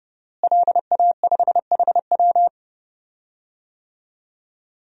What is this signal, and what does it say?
Morse code, heard as a keyed tone, about 28 wpm, 700 Hz. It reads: LA5HW
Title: DXCC-100 Call Signs - 5 Characters - Repeated 3x - 30wpm